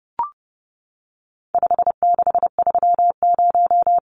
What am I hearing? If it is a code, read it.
Morse code, heard as a keyed tone, about 30 wpm, 700 Hz. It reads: 5630
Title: Numbers - 4-Digit 30wpm